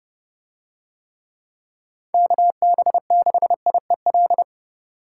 Morse code, heard as a keyed tone, about 30 wpm, 700 Hz. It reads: KB6IEL